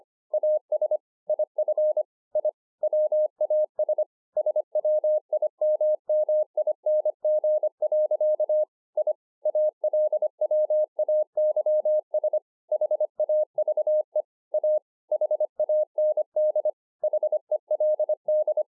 Morse code, heard as a keyed tone, about 25 wpm, 610 Hz. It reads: AS IF I WAS SWIMMING. I ALWAYS HAVE A HAND HELD